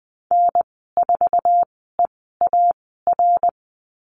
Morse code, heard as a keyed tone, about 20 wpm, spaced as for 14 wpm, 700 Hz. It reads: N4EAR
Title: 062 Lesson 11g - Letter h - Call Signs 20wpm